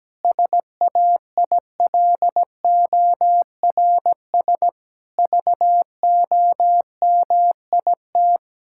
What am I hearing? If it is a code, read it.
Morse code, heard as a keyed tone, about 17 wpm, 700 Hz. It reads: SAILORS VOMIT
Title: Pangram Challenge v2 - 17wpm